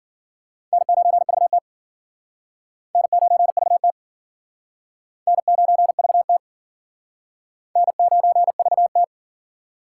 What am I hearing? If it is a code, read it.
Morse code, heard as a keyed tone, about 49 wpm, 700 Hz. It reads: N9VT N9VT N9VT N9VT
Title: Speed Racing - USA 1x2 Call Signs 40wpm